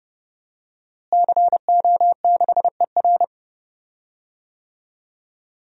Morse code, heard as a keyed tone, about 30 wpm, 700 Hz. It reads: CO6ER
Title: DXCC-100 Call Signs - 5 Characters - Repeated 3x - 30wpm